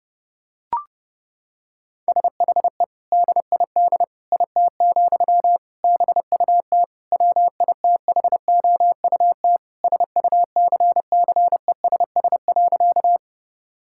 Morse code, heard as a keyed tone, about 32 wpm, 700 Hz. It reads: SHE DID IT, BUT WITHOUT SUCCESS.
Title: Sentences from Top 1000 Words - Review Part 6 - 30wpm